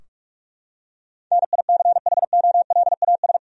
Morse code, heard as a keyed tone, about 46 wpm, 700 Hz. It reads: NICHOLAS
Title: USA Common Names - 45wpm